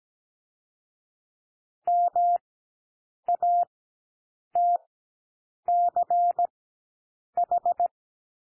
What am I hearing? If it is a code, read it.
Morse code, heard as a keyed tone, about 17 wpm, spaced as for 7 wpm, 700 Hz. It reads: MATCH
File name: SOUND LETTERS 17 wpm